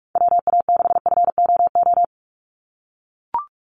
Morse code, heard as a keyed tone, about 44 wpm, 700 Hz. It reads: WA6FOO
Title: USA Call Signs 45wpm